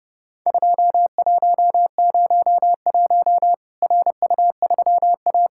Morse code, heard as a keyed tone, about 30 wpm, 700 Hz. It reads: 2101 RU3A
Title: Worked All Europe DX Contest - QTC Exchange - 30wpm